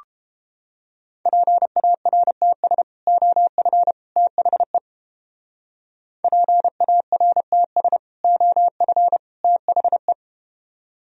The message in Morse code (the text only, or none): PARTS OF THE PARTS OF THE